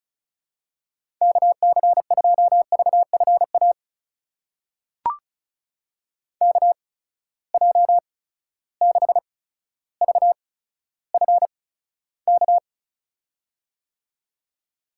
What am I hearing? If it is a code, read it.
Morse code, heard as a keyed tone, about 36 wpm, spaced as for 15 wpm, 700 Hz. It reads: KC2VFA KJ6VFK
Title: USA Call Signs - 6 Characters - 8x Character Spacing - 35wpm